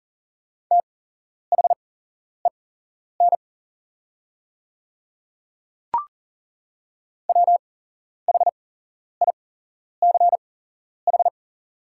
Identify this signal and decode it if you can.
Morse code, heard as a keyed tone, about 39 wpm, spaced as for 11 wpm, 700 Hz. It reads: THEN WHICH
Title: 8x Character Spacing - Top 100 Words - 40wpm